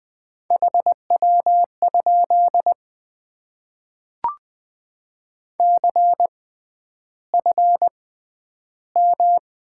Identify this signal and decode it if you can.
Morse code, heard as a keyed tone, about 20 wpm, spaced as for 10 wpm, 700 Hz. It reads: HW? CFM
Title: Top 90 QSO Elements - 6x Character Spacing - 20wpm